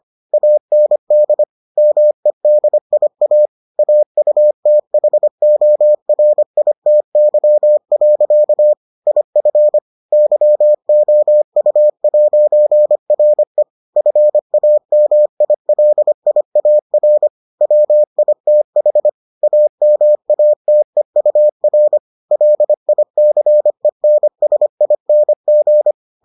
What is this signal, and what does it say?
Morse code, heard as a keyed tone, about 25 wpm, 600 Hz. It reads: AND MEDIA AUTHORITY. IF YOU'RE FAMILIAR WITH AMATEUR LICENSING